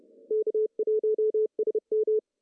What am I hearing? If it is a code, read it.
Morse code, heard as a keyed tone, about 30 wpm, 440 Hz. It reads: K1SM
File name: CW / cqww-cw-k5tr